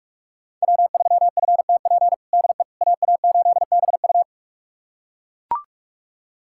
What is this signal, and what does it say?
Morse code, heard as a keyed tone, about 41 wpm, 700 Hz. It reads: W3FTP DE AA8BU
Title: USA Callsign DE Callsign - 5 Characters - 45wpm